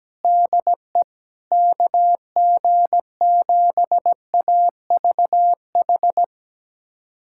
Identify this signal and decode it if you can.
Morse code, heard as a keyed tone, about 17 wpm, 700 Hz. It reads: DE KG7AVH
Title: USA Callsign DE Callsign 17wpm